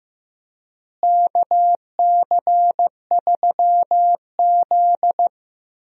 Morse code, heard as a keyed tone, about 15 wpm, 700 Hz. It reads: KC3Z
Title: USA Call Signs - 4 Characters - Repeated 3x - 15wpm